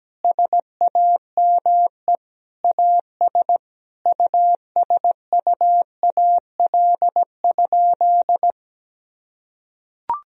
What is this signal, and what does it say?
Morse code, heard as a keyed tone, about 17 wpm, 700 Hz. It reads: SAME AS USUAL?
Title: Sentences from Top 1000 Words - Review Part 2 - 17wpm